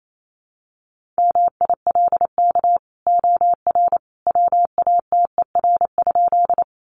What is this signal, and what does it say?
Morse code, heard as a keyed tone, about 28 wpm, 700 Hz. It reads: MILK OR WATER?
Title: Sentences from Top 1500 Words 28wpm